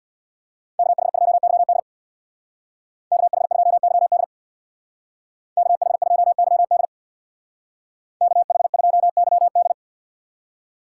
Morse code, heard as a keyed tone, about 64 wpm, 700 Hz. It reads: KH2YD KH2YD KH2YD KH2YD